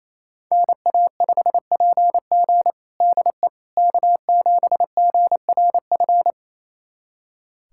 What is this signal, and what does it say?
Morse code, heard as a keyed tone, about 28 wpm, 700 Hz. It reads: NA5PG DE K7GRF
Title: USA Callsign DE Callsign - 5 Characters - 28wpm